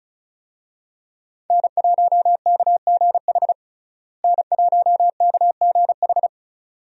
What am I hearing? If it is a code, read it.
Morse code, heard as a keyed tone, about 36 wpm, 700 Hz. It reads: N1KGH N1KGH
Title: USA Call Signs - 5 Characters - Repeated 2x - 35wpm